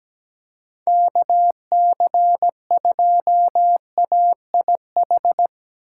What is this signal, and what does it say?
Morse code, heard as a keyed tone, about 17 wpm, 700 Hz. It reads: KC2AIH
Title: USA Call Signs - 6 Characters - 4x Character Spacing - 17wpm